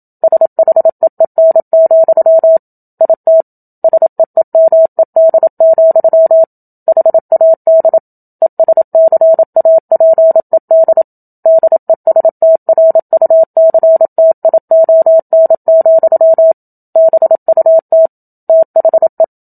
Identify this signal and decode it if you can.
Morse code, heard as a keyed tone, about 27 wpm, 650 Hz. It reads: SHEEN, IT SEEMED, HAD ESCAPED DESTRUCTION, BUT THE
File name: Book Two / 01-Chapter 1 - 27 WPM